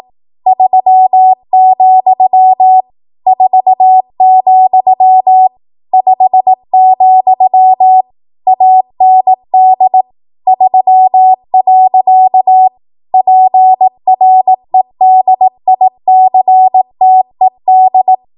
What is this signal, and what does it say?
Morse code, heard as a keyed tone, about 18 wpm, 750 Hz. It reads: 3, 4, 5, AND 3. PREDICTED
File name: Morse / 18WPM